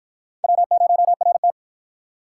Morse code, heard as a keyed tone, about 53 wpm, 700 Hz. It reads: W0RT